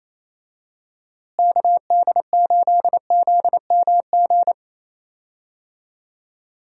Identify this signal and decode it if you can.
Morse code, heard as a keyed tone, about 28 wpm, 700 Hz. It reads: KD8ZMG
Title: USA Call Signs - 6 Characters - Repeated 2x - 28wpm